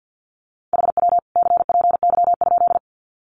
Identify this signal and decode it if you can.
Morse code, heard as a keyed tone, about 50 wpm, 700 Hz. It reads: HW CPY?
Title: Common QSO Phrases 50wpm